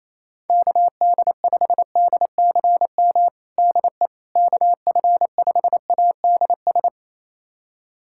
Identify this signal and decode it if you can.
Morse code, heard as a keyed tone, about 28 wpm, 700 Hz. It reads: KD5DCM DE KF5ADS
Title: USA Callsign DE Callsign 28wpm